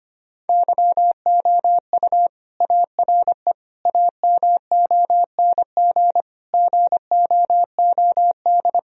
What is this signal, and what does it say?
Morse code, heard as a keyed tone, about 25 wpm, 700 Hz. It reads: YOU ARE AMONG GOOD
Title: Sentences from Top 1000 Words - Review Part 3 - 25wpm